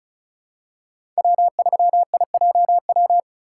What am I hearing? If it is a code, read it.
Morse code, heard as a keyed tone, about 34 wpm, 700 Hz. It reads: W3IJW
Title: USA Call Signs - 5 Characters - 3x Character Spacing - 35wpm